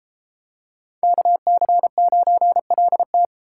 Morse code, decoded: KC9LT

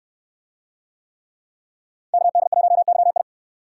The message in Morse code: KD1YI

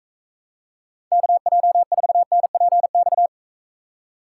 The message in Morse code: KJ4NPX

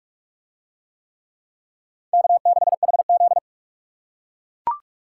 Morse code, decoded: K6HZ

